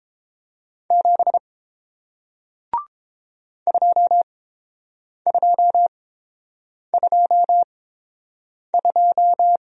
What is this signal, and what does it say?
Morse code, heard as a keyed tone, about 30 wpm, spaced as for 7 wpm, 700 Hz. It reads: 7 2222